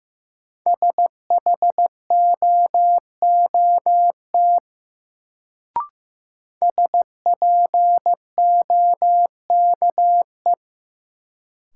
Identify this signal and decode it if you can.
Morse code, heard as a keyed tone, about 15 wpm, 700 Hz. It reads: SHOOT SPOKE
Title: Sentences from Top 1200 Words - New Word Drill - 15wpm